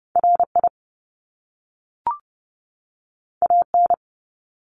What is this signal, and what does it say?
Morse code, heard as a keyed tone, about 30 wpm, 700 Hz. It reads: RI AN
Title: Top Two Letter Combinations in Words 30wpm